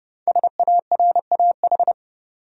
Morse code, heard as a keyed tone, about 30 wpm, 700 Hz. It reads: SARAH
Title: USA Common Names - Max 5 Letters - 30wpm